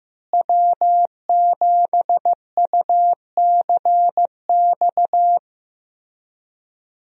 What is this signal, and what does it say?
Morse code, heard as a keyed tone, about 15 wpm, 700 Hz. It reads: W7UCX
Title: USA Call Signs - 5 Characters - Repeated 3x - 15wpm